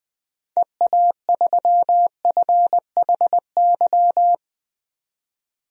Morse code, heard as a keyed tone, about 20 wpm, 700 Hz. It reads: EA3FHY
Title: DXCC-100 Call Signs - 6 Characters - 20wpm